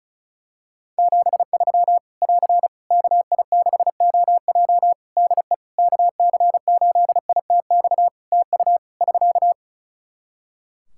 Morse code, heard as a keyed tone, about 34 wpm, 700 Hz. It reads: 73 <AR> KI6OJ DE KC8ITX TU <SK>
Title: K1USN SST QSO Practice 35wpm